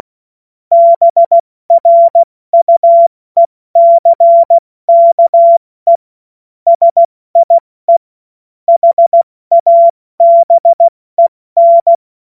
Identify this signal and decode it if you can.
Morse code, heard as a keyed tone, about 16 wpm, spaced as for 14 wpm, 680 Hz. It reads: BRUECKE SIE HABEN
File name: Morsekurs / Witze / Witz 26. Oktober 2021, 70 F80